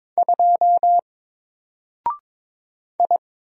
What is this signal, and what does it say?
Morse code, heard as a keyed tone, about 22 wpm, 700 Hz. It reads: 2 I
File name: Single-Letter-Number-22wpm